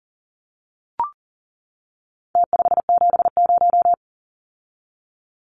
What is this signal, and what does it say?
Morse code, heard as a keyed tone, about 40 wpm, 700 Hz. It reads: T570
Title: Numbers - Contest Style - 4-Digit 40wpm